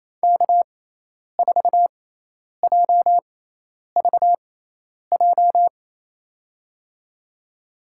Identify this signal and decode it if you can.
Morse code, heard as a keyed tone, about 27 wpm, spaced as for 10 wpm, 700 Hz. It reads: K4JVJ
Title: USA Call Signs - 5 Characters - 6x Character Spacing - 28wpm